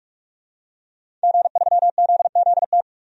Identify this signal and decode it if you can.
Morse code, heard as a keyed tone, about 46 wpm, 700 Hz. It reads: G3ZZT
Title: DXCC-100 Call Signs - 5 Characters - 45wpm